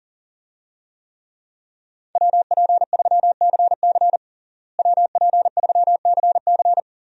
Morse code, heard as a keyed tone, about 40 wpm, 700 Hz. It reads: WP3CC WP3CC
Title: USA Call Signs - 5 Characters - Repeated 2x - 40wpm